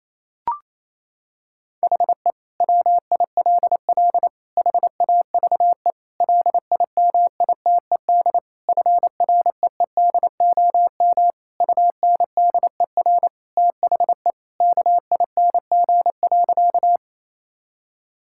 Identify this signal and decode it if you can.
Morse code, heard as a keyed tone, about 28 wpm, 700 Hz. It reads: HE WILL HAVE LIMITED FREEDOM UNDER THE KING.
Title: Sentences from Top 1000 Words - Review Part 6 - 28wpm